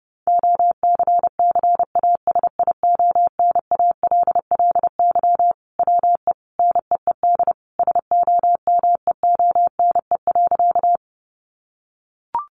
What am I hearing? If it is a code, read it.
Morse code, heard as a keyed tone, about 30 wpm, 700 Hz. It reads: OCCASIONALLY WE NEED SOMEONE.
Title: Sentences from Top 1300 Words 30wpm